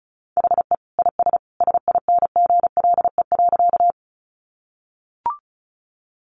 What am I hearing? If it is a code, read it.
Morse code, heard as a keyed tone, about 34 wpm, 700 Hz. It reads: HE IS SINGLE.
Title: Sentences from Top 700 Words 35wpm